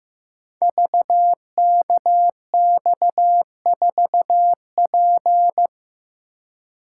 Morse code, heard as a keyed tone, about 15 wpm, 700 Hz. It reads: VKX4P
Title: Code Groups - 5 Letter:Numbers - 15wpm